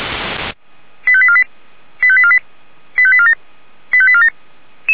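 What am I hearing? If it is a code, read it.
Five-tone selective call (ZVEI), heard as quick runs of five tones, tones of about 70 ms. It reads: 85848 85848 85847 85847
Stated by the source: ZVEI five-tone call 89494 received by SDR, feuerwehr-orlamuende (fire brigade Orlamünde).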